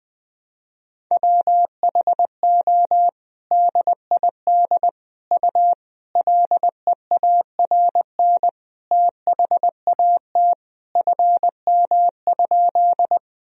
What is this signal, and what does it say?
Morse code, heard as a keyed tone, about 21 wpm, 700 Hz. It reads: WHO DID U LEARN THAT FM?
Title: Sentences from Top 300 Words and Abbreviations 20wpm